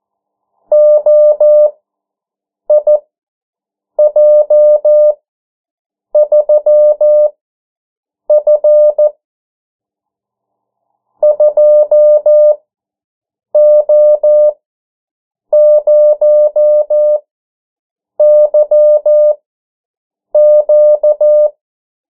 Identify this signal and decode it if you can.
Morse code, heard as a keyed tone, about 14 wpm, spaced as for 7 wpm, 600 Hz. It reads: OIJ3F 2O0YQ
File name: morse mp3 / 09 urok 8